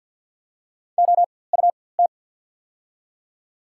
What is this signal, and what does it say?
Morse code, heard as a keyed tone, about 54 wpm, spaced as for 24 wpm, 690 Hz. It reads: OUT
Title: 4x Character Spacing - Top 100 Words - 50wpm